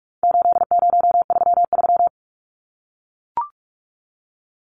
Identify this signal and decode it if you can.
Morse code, heard as a keyed tone, about 44 wpm, 700 Hz. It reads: 8033
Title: Numbers - 4-Digit 45wpm